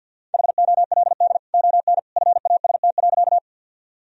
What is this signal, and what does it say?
Morse code, heard as a keyed tone, about 52 wpm, 700 Hz. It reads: HOLD ON FAST.
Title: Sentences from Top 1000 Words - Review Part 2 - 50wpm